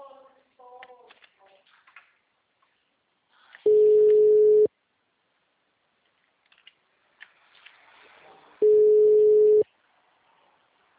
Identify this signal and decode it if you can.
Ringback tone (Europe, 425 Hz).
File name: out- -2012- - - .15963